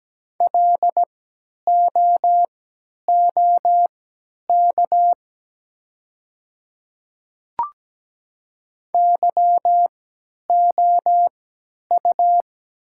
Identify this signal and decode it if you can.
Morse code, heard as a keyed tone, about 17 wpm, spaced as for 10 wpm, 700 Hz. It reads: LOOK YOU
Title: 3x Character Spacing - Top 200 Words - 17wpm